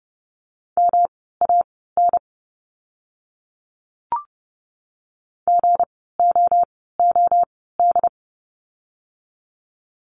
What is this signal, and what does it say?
Morse code, heard as a keyed tone, about 30 wpm, spaced as for 17 wpm, 700 Hz. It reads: MAN GOOD
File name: course / 30:10wpm / 237 Lesson 38f - Letter q - Words 30wpm